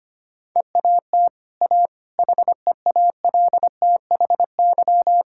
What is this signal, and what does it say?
Morse code, heard as a keyed tone, about 25 wpm, 700 Hz. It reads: EAT A HEALTHY
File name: Easy Sentences with Hints - 25wpm